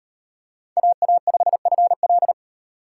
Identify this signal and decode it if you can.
Morse code, heard as a keyed tone, about 37 wpm, 700 Hz. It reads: AA5FL